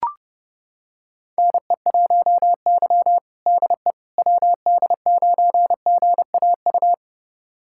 Morse code, heard as a keyed tone, about 30 wpm, 700 Hz. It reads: NE1Y DE WD9GAU
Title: USA Callsign DE Callsign 30wpm